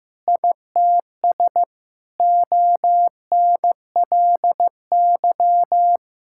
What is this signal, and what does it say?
Morse code, heard as a keyed tone, about 15 wpm, 700 Hz. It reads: ITS ONLY
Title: Sentences from Top 1000 Words - Review Part 5 - 15wpm